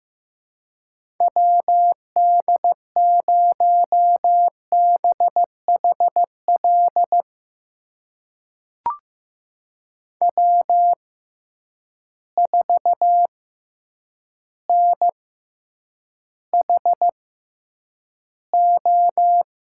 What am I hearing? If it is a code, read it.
Morse code, heard as a keyed tone, about 16 wpm, spaced as for 8 wpm, 700 Hz. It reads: WD0BHL W4NHO
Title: USA Call Signs - 6x Character Spacing - 15wpm